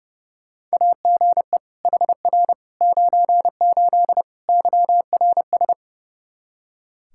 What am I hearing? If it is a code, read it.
Morse code, heard as a keyed tone, about 30 wpm, 700 Hz. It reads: AGE HR 98 YRS
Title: K1USN SST QSO Practice 30wpm